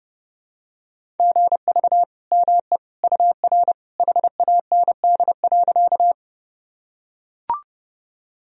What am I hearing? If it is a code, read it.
Morse code, heard as a keyed tone, about 30 wpm, 700 Hz. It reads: GV ME UR HAND.
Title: Sentences from Top 200 Words and Abbreviations 30wpm